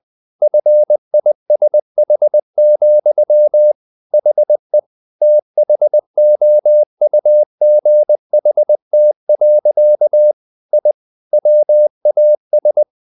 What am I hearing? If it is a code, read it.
Morse code, heard as a keyed tone, about 20 wpm, 600 Hz. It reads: FISH, HE THOUGHT. I WAS
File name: the old man and the sea / the old man and the sea part 1 20wpm